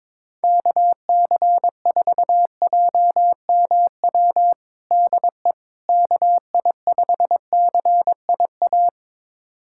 Morse code, heard as a keyed tone, about 22 wpm, 700 Hz. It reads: KC4JMW DE KI5CIA
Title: USA Callsign DE Callsign 22wpm